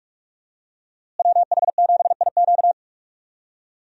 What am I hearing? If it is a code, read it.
Morse code, heard as a keyed tone, about 42 wpm, 700 Hz. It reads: WH7IQ
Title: USA Call Signs - 5 Characters - Repeated 3x - 45wpm